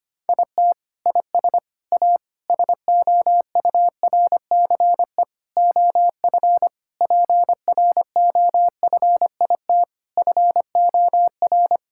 Morse code, heard as a keyed tone, about 25 wpm, 700 Hz. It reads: IT IS A SOURCE OF PROFIT FOR
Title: Sentences from Top 1000 Words - Review Part 5 - 25wpm